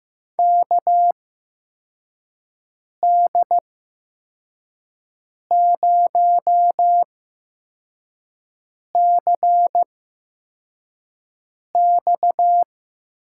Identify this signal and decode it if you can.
Morse code, heard as a keyed tone, about 15 wpm, spaced as for 4 wpm, 700 Hz. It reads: KD0CX
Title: USA Call Signs - 6 Characters - 8x Character Spacing - 15wpm